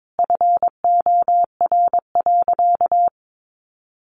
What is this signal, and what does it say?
Morse code, heard as a keyed tone, about 22 wpm, 700 Hz. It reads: FOR.
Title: Sentences from Top 1400 Words 22wpm